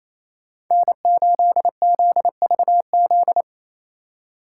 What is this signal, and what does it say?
Morse code, heard as a keyed tone, about 36 wpm, 700 Hz. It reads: N8ZVZ